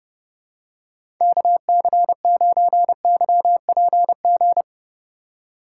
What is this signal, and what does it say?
Morse code, heard as a keyed tone, about 30 wpm, 700 Hz. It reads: KC9YPG